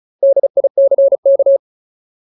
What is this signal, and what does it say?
Morse code, heard as a keyed tone, about 34 wpm, 550 Hz. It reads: DICK
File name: Ultimate Word List / 35WPM Ultimate